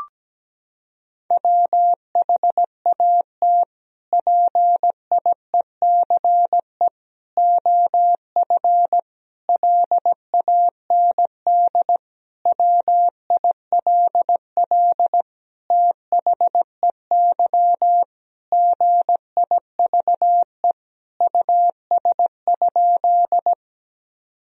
Morse code, heard as a keyed tone, about 17 wpm, 700 Hz. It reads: WHAT PIECE OF LAND WILL THEY GIVE US?
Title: Sentences from Top 1000 Words - Review Part 2 - 17wpm